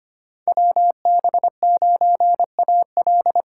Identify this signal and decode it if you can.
Morse code, heard as a keyed tone, about 25 wpm, 700 Hz. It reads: WB9AL
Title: USA Call Signs - 5 Characters - 2x Character Spacing - 25wpm